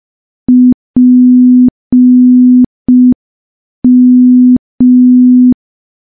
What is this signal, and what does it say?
Morse code, heard as a keyed tone, about 5 wpm, 250 Hz. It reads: PM
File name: morse / morse-training-05WPM